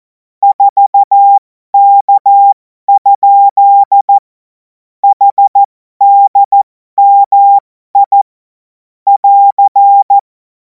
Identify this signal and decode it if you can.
Morse code, heard as a keyed tone, about 14 wpm, spaced as for 12 wpm, 800 Hz. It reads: 4K? HDMI <AR>
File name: Morsekurs / Witze